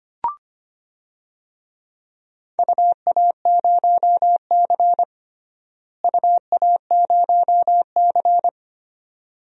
Morse code, heard as a keyed tone, about 25 wpm, 700 Hz. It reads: UA0C UA0C
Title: DXCC-100 Call Signs - 4 Characters - Repeated 2x - 25wpm